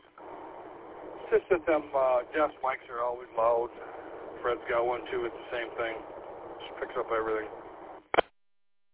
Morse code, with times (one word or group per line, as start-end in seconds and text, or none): none